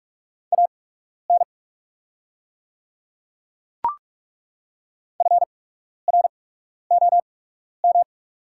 Morse code, heard as a keyed tone, about 42 wpm, spaced as for 12 wpm, 700 Hz. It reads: AN FROM
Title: 8x Character Spacing - Top 100 Words - 45wpm